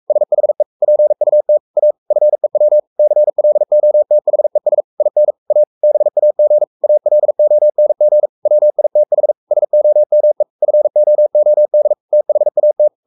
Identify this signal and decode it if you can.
Morse code, heard as a keyed tone, about 44 wpm, 600 Hz. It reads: SHE PUT A FEW CLOTHES IN A BAG ALONG WITH SOME FOOD THAT
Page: Romanen Never af Ken Follet på engelsk.